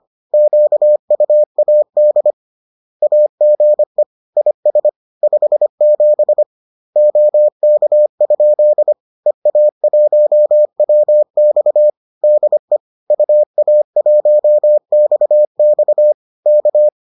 Morse code, heard as a keyed tone, about 25 wpm, 600 Hz. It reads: QUAD AGE IS 57 OK? EA1WX DE UA1XX K